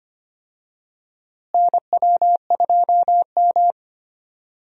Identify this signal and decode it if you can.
Morse code, heard as a keyed tone, about 25 wpm, 700 Hz. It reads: NW2M
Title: USA Call Signs - 4 Characters - Repeated 2x - 25wpm